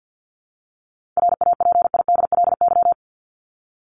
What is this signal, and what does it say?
Morse code, heard as a keyed tone, about 50 wpm, 700 Hz. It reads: RAPIDLY
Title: LY Suffix - Top 100 Words with Base Words 50wpm